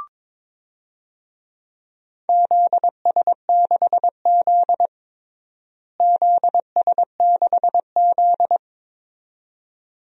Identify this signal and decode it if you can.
Morse code, heard as a keyed tone, about 22 wpm, 700 Hz. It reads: ZS6Z ZS6Z
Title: DXCC-100 Call Signs - 4 Characters - Repeated 2x - 22wpm